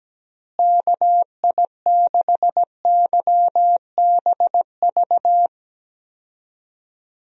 Morse code, heard as a keyed tone, about 17 wpm, 700 Hz. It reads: KI6YBV
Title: USA Call Signs - 6 Characters - Repeated 3x - 17wpm